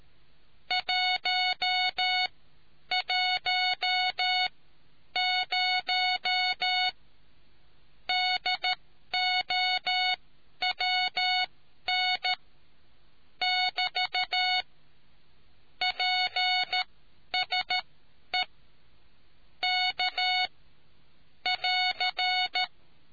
Morse code, heard as a keyed tone, about 13 wpm, spaced as for 10 wpm, 740 Hz. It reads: 110 DOWN <BT> PSE K <AR>